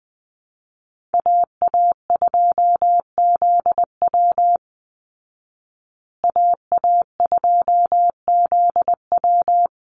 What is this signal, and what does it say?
Morse code, heard as a keyed tone, about 20 wpm, 700 Hz. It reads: AA2ZW AA2ZW
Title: Call Signs Repeated 2x - 20wpm